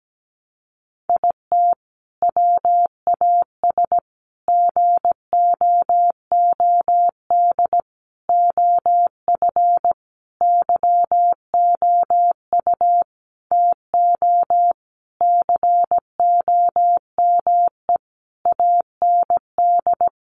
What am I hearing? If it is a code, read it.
Morse code, heard as a keyed tone, about 17 wpm, 700 Hz. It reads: IT WAS GOOD OF YOU TO COME AND
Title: Sentences from Top 1100 Words 17wpm